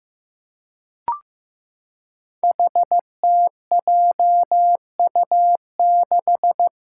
Morse code, heard as a keyed tone, about 15 wpm, 700 Hz. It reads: HTJU6
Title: Five Challenging Letter-Number Combinations 15wpm